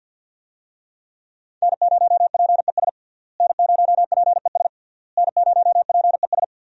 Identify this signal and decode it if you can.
Morse code, heard as a keyed tone, about 48 wpm, 700 Hz. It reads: N0PES N0PES N0PES
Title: USA Call Signs - 5 Characters - Repeated 3x - 50wpm